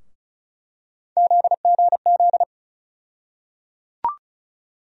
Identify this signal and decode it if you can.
Morse code, heard as a keyed tone, about 36 wpm, 700 Hz. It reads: ZGZ
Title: USA Call Sign Postfixes - 3 Letters 35wpm